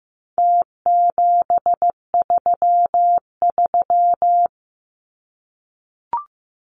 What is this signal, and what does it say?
Morse code, heard as a keyed tone, about 15 wpm, 700 Hz. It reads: T733
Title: Numbers - Contest Style - 4-Digit 15wpm